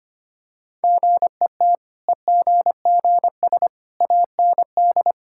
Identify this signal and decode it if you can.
Morse code, heard as a keyed tone, about 25 wpm, 690 Hz. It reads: GET EGGS AND
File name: Easy Sentences with Hints - 25wpm